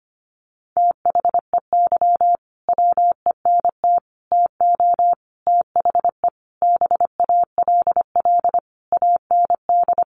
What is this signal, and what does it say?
Morse code, heard as a keyed tone, about 25 wpm, 700 Hz. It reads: THEY WENT TO THE BALL AND
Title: Sentences from Top 900 Words 25wpm